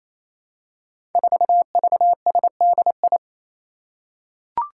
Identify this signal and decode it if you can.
Morse code, heard as a keyed tone, about 27 wpm, 700 Hz. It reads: 4VSDI
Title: Five Challenging Letter-Number Combinations 28wpm